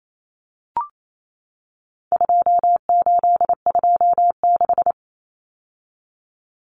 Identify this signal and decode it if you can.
Morse code, heard as a keyed tone, about 28 wpm, 700 Hz. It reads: 2826
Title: Numbers - 4-Digit 28wpm